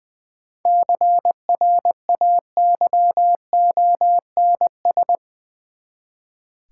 Morse code, heard as a keyed tone, about 20 wpm, 700 Hz. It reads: CRAYONS
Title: Easy Words with Hints - 20wpm